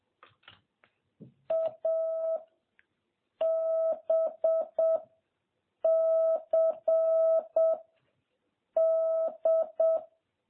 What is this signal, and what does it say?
Morse code, heard as a keyed tone, about 7 wpm, 660 Hz. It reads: A B C D
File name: jh8cbh / 2021cw